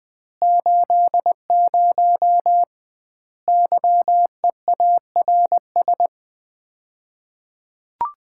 Morse code, heard as a keyed tone, about 21 wpm, 700 Hz. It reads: 80 YEARS
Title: Common QSO Phrases - 2x Word-Spacing - 20wpm